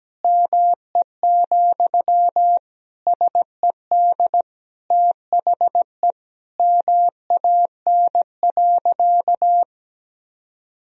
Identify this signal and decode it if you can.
Morse code, heard as a keyed tone, about 17 wpm, 700 Hz. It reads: ME, SED THE MAN.
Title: Sentences from Top 200 Words and Abbreviations 17wpm